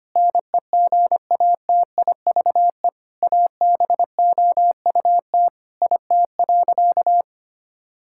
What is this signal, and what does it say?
Morse code, heard as a keyed tone, about 25 wpm, 700 Hz. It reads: NEGATIVE ABOUT IT.
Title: Sentences from Top 1000 Words - Review Part 6 - 25wpm